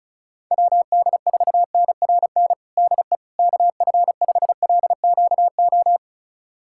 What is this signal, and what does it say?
Morse code, heard as a keyed tone, about 36 wpm, 700 Hz. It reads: WD4NRN DE KF5LQO
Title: USA Callsign DE Callsign - 6 Characters - 35wpm